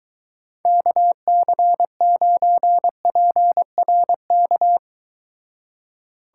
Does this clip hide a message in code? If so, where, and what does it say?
Yes: Morse code from 0.7 to 4.8 s, heard as a keyed tone, about 23 wpm, 700 Hz. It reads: KC9PRK